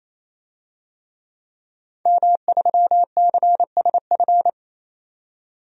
Morse code, heard as a keyed tone, about 28 wpm, 700 Hz. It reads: M3CSF
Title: DXCC-100 Call Signs - 5 Characters - 28wpm